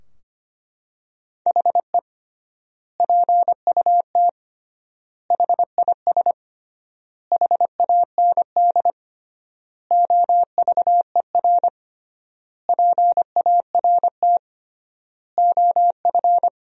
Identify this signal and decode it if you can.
Morse code, heard as a keyed tone, about 25 wpm, 700 Hz. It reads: HE PUT HIS HAND OVER PART OF
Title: 3x Word-Spacing - Sentences from Top 400 Words - 25wpm